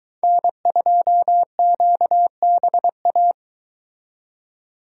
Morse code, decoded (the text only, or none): N2QBA